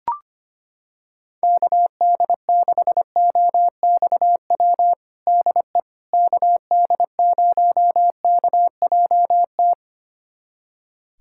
Morse code, heard as a keyed tone, about 25 wpm, 700 Hz. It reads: KD6OXW DE KD0KJT
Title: USA Callsign DE Callsign - 6 Characters - 25wpm